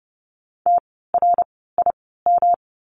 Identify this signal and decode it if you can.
Morse code, heard as a keyed tone, about 30 wpm, spaced as for 17 wpm, 700 Hz. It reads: TRIM
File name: course / 30:10wpm / 108 Lesson 18f - Letter m - Words 30wpm